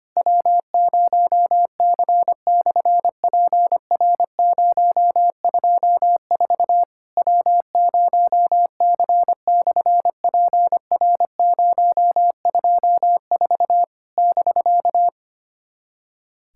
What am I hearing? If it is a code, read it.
Morse code, heard as a keyed tone, about 25 wpm, 700 Hz. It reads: W0C/PR024 W0C/PR024 <BK>
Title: SOTA QSO Practice 25wpm